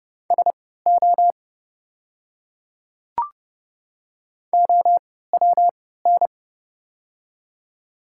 Morse code, heard as a keyed tone, about 30 wpm, spaced as for 17 wpm, 700 Hz. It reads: SO OWN
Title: 3x Character Spacing - Top 300 Words - 30wpm